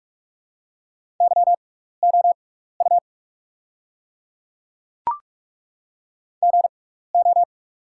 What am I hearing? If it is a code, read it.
Morse code, heard as a keyed tone, about 48 wpm, spaced as for 16 wpm, 690 Hz. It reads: YOU GO
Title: 6x Character Spacing - Top 500 Words - 45wpm